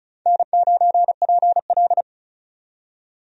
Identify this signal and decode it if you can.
Morse code, heard as a keyed tone, about 34 wpm, 700 Hz. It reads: N9PL
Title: USA Call Signs - 4 Characters - 35wpm